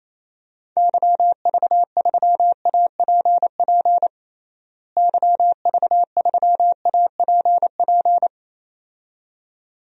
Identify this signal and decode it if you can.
Morse code, heard as a keyed tone, about 28 wpm, 700 Hz. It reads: YV3APP YV3APP
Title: DXCC-100 Call Signs - 6 Characters - Repeated 2x - 28wpm